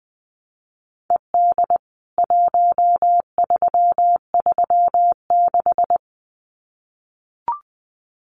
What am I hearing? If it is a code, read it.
Morse code, heard as a keyed tone, about 20 wpm, 700 Hz. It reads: ED 1336
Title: CWT Exchange 20wpm